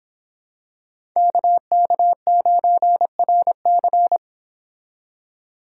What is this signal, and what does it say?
Morse code, heard as a keyed tone, about 26 wpm, 700 Hz. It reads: KK9RC